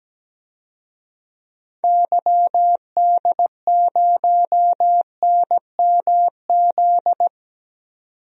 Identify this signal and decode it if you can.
Morse code, heard as a keyed tone, about 17 wpm, 700 Hz. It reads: YD0NMZ
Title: DXCC-100 Call Signs - 6 Characters - 17wpm